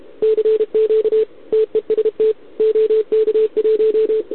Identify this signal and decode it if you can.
Morse code, heard as a keyed tone, about 31 wpm, 440 Hz, with radio noise. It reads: CQ TEST OK1
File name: ARCHIV / OKDX2020 / OK1RIb